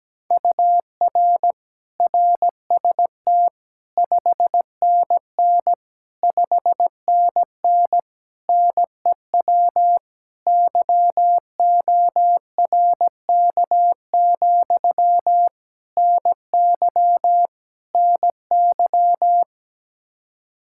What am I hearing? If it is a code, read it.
Morse code, heard as a keyed tone, about 17 wpm, 700 Hz. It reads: UR RST 5NN 5NN NEW YORK, NY NY
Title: Practice QSO - Exchange 1 - Beginning - 17wpm